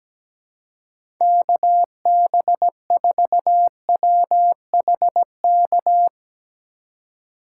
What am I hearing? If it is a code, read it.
Morse code, heard as a keyed tone, about 17 wpm, 700 Hz. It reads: KB4WHK
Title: USA Call Signs - 6 Characters - 2x Character Spacing - 17wpm